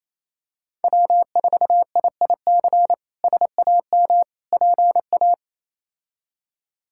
Morse code, heard as a keyed tone, about 27 wpm, 700 Hz. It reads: W4IIC SAM PA
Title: K1USN SST QSO Practice - 28wpm